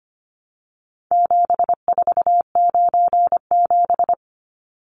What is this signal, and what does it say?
Morse code, heard as a keyed tone, about 25 wpm, 700 Hz. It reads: 7497
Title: Numbers - 4-Digit 25wpm